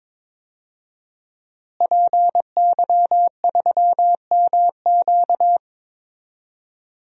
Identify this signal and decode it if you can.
Morse code, heard as a keyed tone, about 21 wpm, 700 Hz. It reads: PY3MQ
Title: DXCC-100 Call Signs - 5 Characters - Repeated 2x - 22wpm